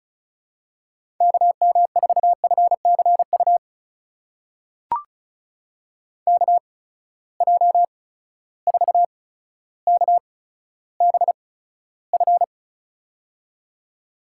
Morse code, heard as a keyed tone, about 36 wpm, spaced as for 15 wpm, 700 Hz. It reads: KM4FCU KJ4KBF
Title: USA Call Signs - 6 Characters - 8x Character Spacing - 35wpm